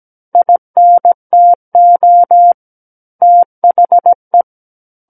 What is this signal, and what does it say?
Morse code, heard as a keyed tone, about 17 wpm, 700 Hz. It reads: INTO THE
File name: Book One / 08-Chapter 8 - 17 WPM